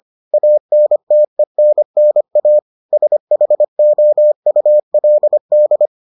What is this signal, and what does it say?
Morse code, heard as a keyed tone, about 25 wpm, 600 Hz. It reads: ANTENNA SHOULD